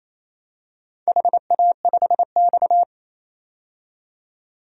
Morse code, heard as a keyed tone, about 28 wpm, 700 Hz. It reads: HA5X
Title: DXCC-100 Call Signs - 4 Characters - 28wpm